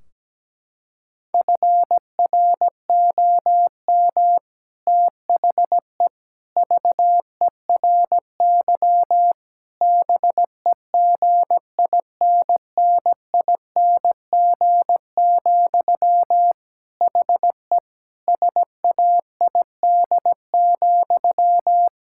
Morse code, heard as a keyed tone, about 17 wpm, 700 Hz. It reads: FROM THE VERY BEGINNING, HE SAID,
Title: Sentences from Top 1600 Words 17wpm